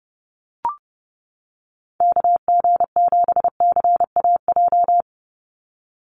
Morse code, heard as a keyed tone, about 30 wpm, 700 Hz. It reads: KG7CAJ
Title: USA Call Signs 30wpm